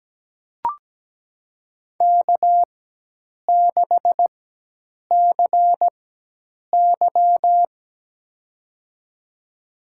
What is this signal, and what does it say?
Morse code, heard as a keyed tone, about 17 wpm, spaced as for 8 wpm, 700 Hz. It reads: K6CY
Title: USA Call Signs - 4 Characters - 4x Character Spacing - 17wpm